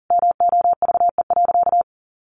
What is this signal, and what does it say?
Morse code, heard as a keyed tone, about 37 wpm, 700 Hz. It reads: MOVE.
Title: Sentences from Top 1200 Words 40wpm